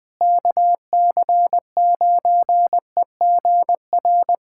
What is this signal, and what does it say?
Morse code, heard as a keyed tone, about 21 wpm, 700 Hz. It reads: KC9EGR
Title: USA Callsign DE Callsign 20wpm